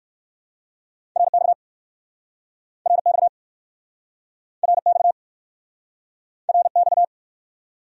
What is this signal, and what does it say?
Morse code, heard as a keyed tone, about 63 wpm, 700 Hz. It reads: RX RX RX RX